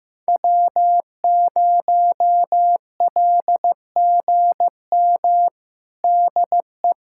Morse code, heard as a keyed tone, about 15 wpm, 700 Hz. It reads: W0LGM DE
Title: USA Callsign DE Callsign - 5 Characters - 15wpm